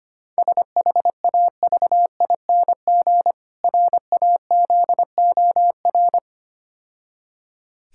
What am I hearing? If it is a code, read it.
Morse code, heard as a keyed tone, about 26 wpm, 700 Hz. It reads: SHAVING RAZOR
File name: practice / 25wpm / Difficult Words - Speed-Racing with Hints - 25wpm